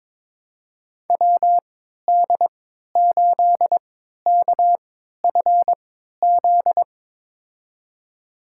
Morse code, heard as a keyed tone, about 22 wpm, spaced as for 13 wpm, 700 Hz. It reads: WD8KFZ
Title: USA Call Signs - 3x Character Spacing - 22wpm